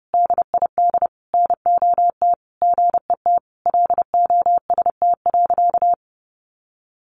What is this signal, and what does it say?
Morse code, heard as a keyed tone, about 30 wpm, 700 Hz. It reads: DID NOT GET LOST.
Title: Sentences from Top 1400 Words 30wpm